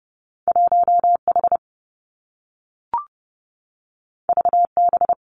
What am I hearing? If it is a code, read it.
Morse code, heard as a keyed tone, about 30 wpm, 700 Hz. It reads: 1 H V B
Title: Two Challenging Letter-Number Combinations 30wpm